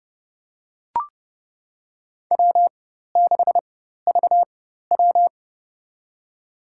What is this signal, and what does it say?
Morse code, heard as a keyed tone, about 30 wpm, spaced as for 14 wpm, 700 Hz. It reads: W6VW
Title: USA Call Signs - 4 Characters - 4x Character Spacing - 30wpm